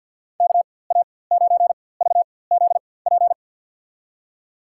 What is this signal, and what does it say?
Morse code, heard as a keyed tone, about 51 wpm, spaced as for 23 wpm, 700 Hz. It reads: KA9VZP
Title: USA Call Signs - 6 Characters - 4x Character Spacing - 50wpm